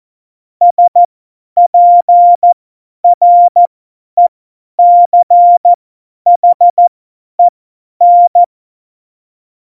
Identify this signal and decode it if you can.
Morse code, heard as a keyed tone, about 14 wpm, spaced as for 10 wpm, 700 Hz. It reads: SPRECHEN
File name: Morsekurs / Witze / Witz 25. Juni 2019, 50 F70